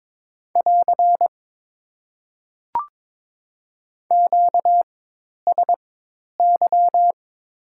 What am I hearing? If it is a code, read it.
Morse code, heard as a keyed tone, about 22 wpm, spaced as for 10 wpm, 700 Hz. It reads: <AR> QSY
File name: Top 90 QSO Elements - 4x Character Spacing - 22wpm